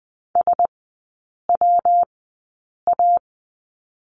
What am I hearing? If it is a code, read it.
Morse code, heard as a keyed tone, about 20 wpm, spaced as for 8 wpm, 700 Hz. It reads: SWA
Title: 114 Lesson 19e - Letter w - Three Characters 20wpm